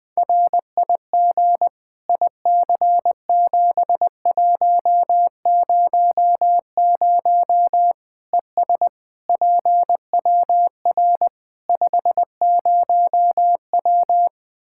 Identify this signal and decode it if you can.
Morse code, heard as a keyed tone, about 20 wpm, 700 Hz. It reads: RIG IC7100 ES PWR 50W
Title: Practice QSO - Exchange 2 - Info - 20wpm